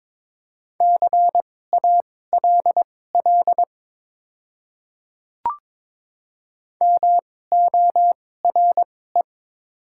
Morse code, heard as a keyed tone, about 22 wpm, spaced as for 16 wpm, 700 Hz. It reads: CALL MORE
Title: 2x Character Spacing - Top 100 Words - 22wpm